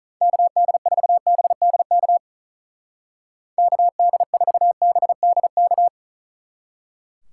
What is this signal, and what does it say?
Morse code, heard as a keyed tone, about 43 wpm, 700 Hz. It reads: KD4BDK KD4BDK
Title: Speed Racing - USA 2x3 Call Signs 35wpm